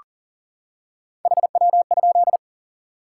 Morse code, heard as a keyed tone, about 40 wpm, 700 Hz. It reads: HW?
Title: Top 90 QSO Elements 40wpm